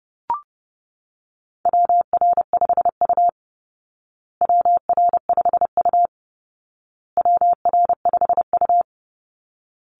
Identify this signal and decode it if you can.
Morse code, heard as a keyed tone, about 30 wpm, 700 Hz. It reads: WR5U WR5U WR5U
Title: Call Signs Repeated 2x - 30wpm